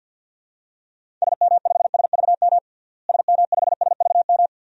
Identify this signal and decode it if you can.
Morse code, heard as a keyed tone, about 49 wpm, 700 Hz. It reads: SM5SVM SM5SVM
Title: DXCC-100 Call Signs - 6 Characters - Repeated 2x - 50wpm